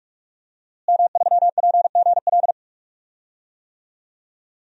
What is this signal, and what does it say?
Morse code, heard as a keyed tone, about 46 wpm, 700 Hz. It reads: M3PGL
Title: DXCC-100 Call Signs - 5 Characters - Repeated 2x - 45wpm